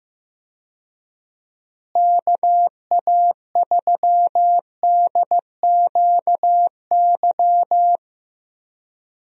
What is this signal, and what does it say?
Morse code, heard as a keyed tone, about 15 wpm, 700 Hz. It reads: KA3DQY
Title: USA Call Signs - 6 Characters - Repeated 3x - 15wpm